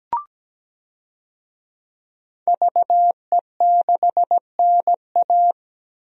Morse code, heard as a keyed tone, about 17 wpm, 700 Hz. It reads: VE6NA
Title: DXCC-100 Call Signs - 5 Characters - Repeated 2x - 17wpm